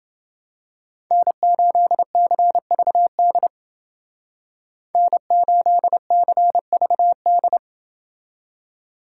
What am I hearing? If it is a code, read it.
Morse code, heard as a keyed tone, about 28 wpm, 700 Hz. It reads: N8CVD N8CVD